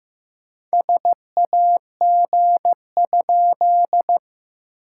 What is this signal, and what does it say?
Morse code, heard as a keyed tone, about 14 wpm, 700 Hz. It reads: SAG?
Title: SOTA QSO Practice 15wpm